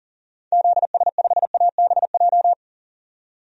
Morse code, heard as a keyed tone, about 40 wpm, 700 Hz. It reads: ZS5ABJ
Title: DXCC-100 Call Signs - 6 Characters - 40wpm